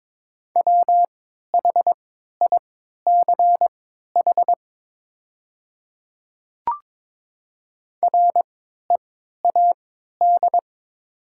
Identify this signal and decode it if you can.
Morse code, heard as a keyed tone, about 22 wpm, spaced as for 12 wpm, 700 Hz. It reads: WHICH READ
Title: 3x Character Spacing - Top 300 Words - 22wpm